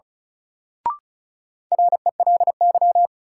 Morse code, heard as a keyed tone, about 34 wpm, 700 Hz. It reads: RELY
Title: Sentences from Top 1200 Words - New Word Drill - 35wpm